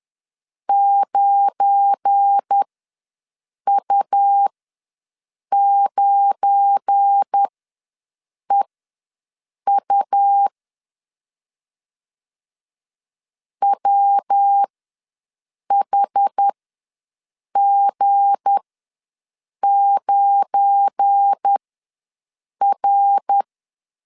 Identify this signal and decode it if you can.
Morse code, heard as a keyed tone, about 10 wpm, spaced as for 6 wpm, 800 Hz. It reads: 9U9EU WHG9R
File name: LEZIONI / 7 GF9Z / 7 GF9 ws 10wpm 4034